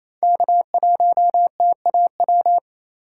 Morse code, heard as a keyed tone, about 28 wpm, 700 Hz. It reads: K1TAW
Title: USA Call Signs - 5 Characters - 2x Character Spacing - 28wpm